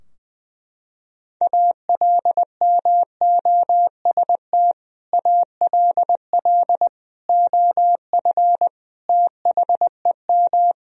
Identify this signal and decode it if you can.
Morse code, heard as a keyed tone, about 20 wpm, 700 Hz. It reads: ALMOST ALL OF THEM
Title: Sentences from Top 1000 Words - Review Part 4 - 20wpm